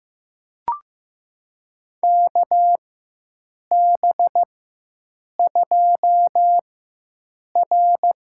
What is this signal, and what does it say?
Morse code, heard as a keyed tone, about 15 wpm, spaced as for 7 wpm, 700 Hz. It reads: KB2R
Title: USA Call Signs - 4x Character Spacing - 15wpm